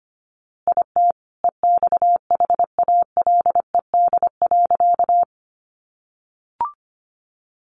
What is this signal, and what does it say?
Morse code, heard as a keyed tone, about 25 wpm, 700 Hz. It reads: IT EXHALED.
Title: A Sound Of Thunder 25wpm